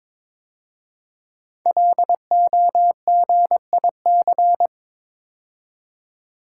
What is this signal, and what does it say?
Morse code, heard as a keyed tone, about 22 wpm, 700 Hz. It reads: LOGIC
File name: 5 Letter Words - First 3 Letters Given - 22wpm